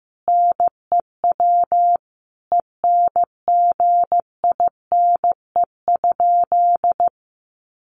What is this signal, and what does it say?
Morse code, heard as a keyed tone, about 15 wpm, 700 Hz. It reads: NEW ENGINE?
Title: Sentences from Top 1500 Words 15wpm